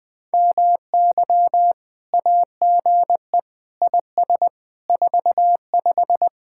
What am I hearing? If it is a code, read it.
Morse code, heard as a keyed tone, about 21 wpm, 700 Hz. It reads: MY AGE IS 45
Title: Speed-Racing Common QSO Phrases Encore - 20wpm